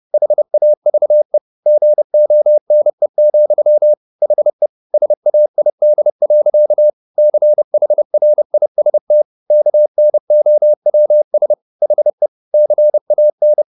Morse code, heard as a keyed tone, about 30 wpm, 600 Hz. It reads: HAVE GONE, HE SAID. CHRIST KNOWS HE CAN
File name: the old man and the sea / the old man and the sea 1 30wpm